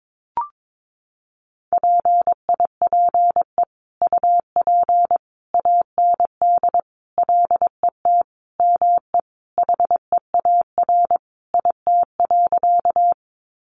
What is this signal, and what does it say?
Morse code, heard as a keyed tone, about 22 wpm, 700 Hz. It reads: PIPE UP AND LET ME HEAR IT.
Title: Sentences from Top 1500 Words 22wpm